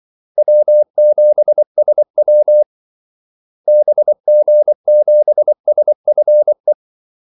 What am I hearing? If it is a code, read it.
Morse code, heard as a keyed tone, about 24 wpm, 600 Hz. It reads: W7SW BG7SFE
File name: Callsigns-24wpm